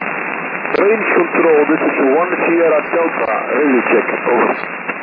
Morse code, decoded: none

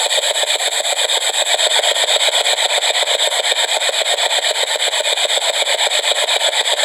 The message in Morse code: none

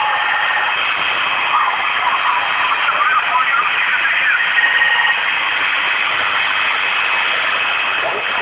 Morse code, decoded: none

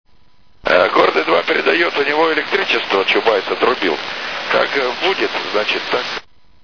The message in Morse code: none

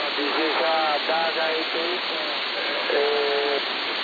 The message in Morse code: none

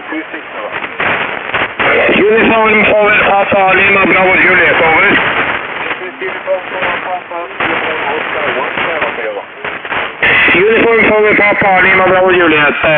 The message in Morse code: none

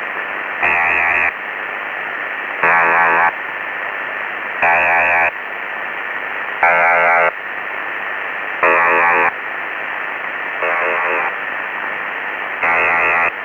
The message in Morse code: none

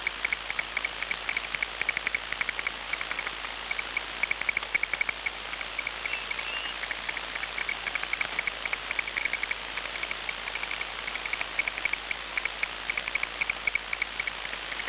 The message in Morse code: none

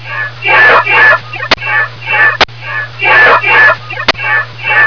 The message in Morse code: none

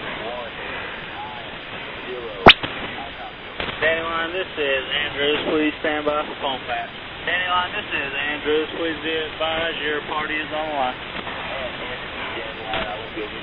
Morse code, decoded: none

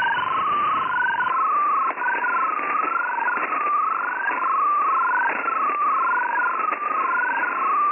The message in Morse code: none

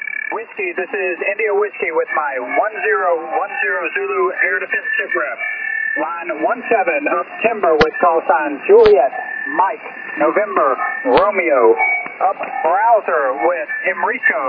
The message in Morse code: none

